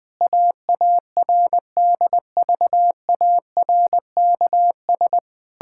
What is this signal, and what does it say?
Morse code, heard as a keyed tone, about 20 wpm, 700 Hz. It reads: AARDVARKS
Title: Pangram Challenge v2 - 20wpm